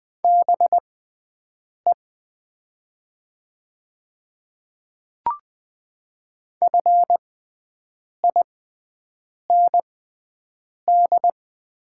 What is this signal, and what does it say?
Morse code, heard as a keyed tone, about 20 wpm, spaced as for 7 wpm, 700 Hz. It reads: BE FIND
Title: 6x Character Spacing - Top 200 Words - 20wpm